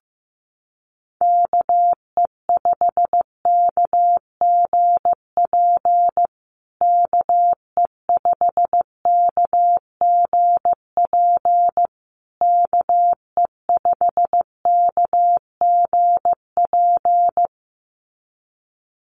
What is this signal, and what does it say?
Morse code, heard as a keyed tone, about 15 wpm, 700 Hz. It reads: KE5KGP KE5KGP KE5KGP
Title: USA Call Signs Repeated 3x 15wpm